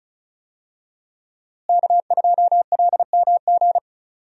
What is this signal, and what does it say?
Morse code, heard as a keyed tone, about 36 wpm, 700 Hz. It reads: K2LMG
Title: USA Call Signs - 5 Characters - Repeated 3x - 35wpm